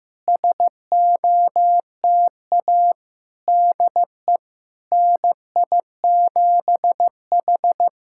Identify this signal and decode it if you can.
Morse code, heard as a keyed tone, about 15 wpm, 700 Hz. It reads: SOTA DE NI7H
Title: SOTA QSO Practice 15wpm